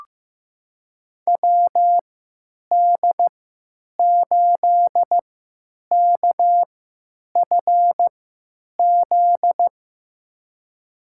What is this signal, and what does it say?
Morse code, heard as a keyed tone, about 15 wpm, spaced as for 9 wpm, 700 Hz. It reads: WD8KFZ
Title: USA Call Signs - 6 Characters - 3x Character Spacing - 15wpm